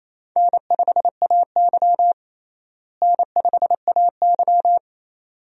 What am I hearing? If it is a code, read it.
Morse code, heard as a keyed tone, about 28 wpm, 700 Hz. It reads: N5AY N5AY
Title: USA Call Signs - 4 Characters - Repeated 2x - 28wpm